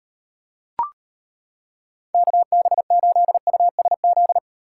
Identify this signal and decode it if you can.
Morse code, heard as a keyed tone, about 32 wpm, 700 Hz. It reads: KB8USZ